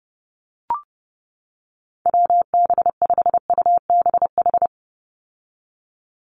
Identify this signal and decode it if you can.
Morse code, heard as a keyed tone, about 30 wpm, 700 Hz. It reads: WB5UBH
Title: USA Call Signs 30wpm